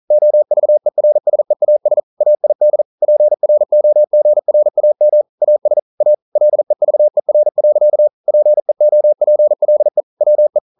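Morse code, heard as a keyed tone, about 40 wpm, 600 Hz. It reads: OVERSEAS AID PROGRAM AS A LEVER. PEOPLE WE
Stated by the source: Morse code practice at 41 wpm, from CW med Gnister.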